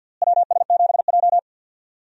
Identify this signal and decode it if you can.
Morse code, heard as a keyed tone, about 48 wpm, 700 Hz. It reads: WS7J
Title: USA Call Signs - 4 Characters - Repeated 3x - 50wpm